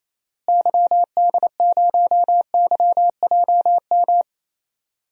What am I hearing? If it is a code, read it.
Morse code, heard as a keyed tone, about 27 wpm, 700 Hz. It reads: YD0YJM
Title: DXCC-100 Call Signs - 6 Characters - Repeated 3x - 28wpm